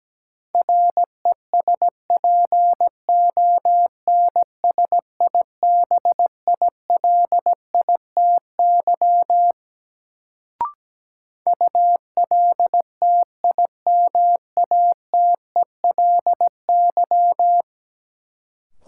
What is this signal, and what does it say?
Morse code, heard as a keyed tone, about 17 wpm, 700 Hz. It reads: RESPONSIBILITY ULTIMATELY
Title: Sentences from Top 1200 Words - New Word Drill Alt - 17wpm